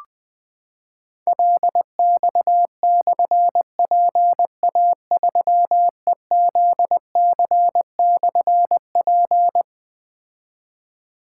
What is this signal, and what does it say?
Morse code, heard as a keyed tone, about 20 wpm, 700 Hz. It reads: LX/PA3EZC/P
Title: Difficult Call Signs 20wpm